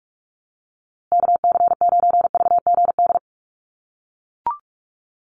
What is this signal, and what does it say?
Morse code, heard as a keyed tone, about 44 wpm, 700 Hz. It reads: KC9VGD
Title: USA Call Signs Repeated 3x 45wpm